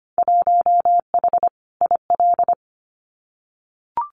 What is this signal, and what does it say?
Morse code, heard as a keyed tone, about 25 wpm, 700 Hz. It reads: 1H IL
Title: Winter Field Day Exchanges 25wpm